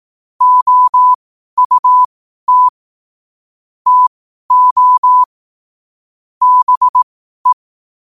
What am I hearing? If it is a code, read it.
Morse code, heard as a keyed tone, about 18 wpm, spaced as for 12 wpm, 1000 Hz. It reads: OUT TO BE
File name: qotdfiles / 12wpm